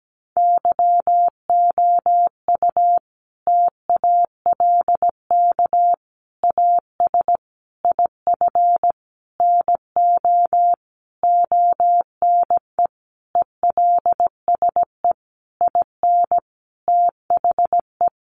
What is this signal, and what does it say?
Morse code, heard as a keyed tone, about 17 wpm, 700 Hz. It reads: YOU TALK AS IF NO ONE ELSE IN THE
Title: Sentences from Top 1100 Words 17wpm